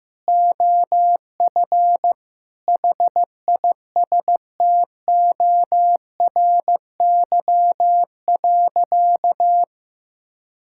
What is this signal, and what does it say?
Morse code, heard as a keyed tone, about 15 wpm, 700 Hz. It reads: OF HISTORY.
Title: Sentences from Top 1000 Words - Review Part 5 - 15wpm